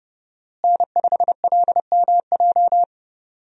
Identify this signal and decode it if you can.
Morse code, heard as a keyed tone, about 30 wpm, 700 Hz. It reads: N5LMJ
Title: USA Call Signs - 5 Characters - Repeated 3x - 30wpm